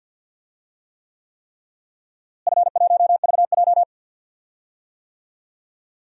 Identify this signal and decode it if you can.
Morse code, heard as a keyed tone, about 46 wpm, 700 Hz. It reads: F1VJ
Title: DXCC-100 Call Signs - 4 Characters - Repeated 3x - 50wpm